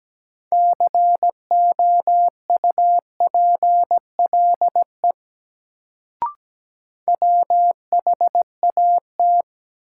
Morse code, heard as a keyed tone, about 17 wpm, 700 Hz. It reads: COUPLE WHAT
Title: Sets of 2 Words - New Word Drill - 17wpm